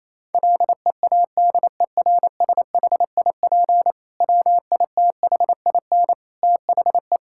Morse code, heard as a keyed tone, about 28 wpm, 700 Hz. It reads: LEADERSHIP WITHIN THE
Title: Sentences from Top 1600 Words 28wpm